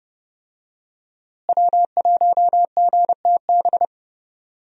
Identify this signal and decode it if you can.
Morse code, heard as a keyed tone, about 29 wpm, 700 Hz. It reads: W1GTB
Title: USA Call Signs - 5 Characters - Repeated 2x - 30wpm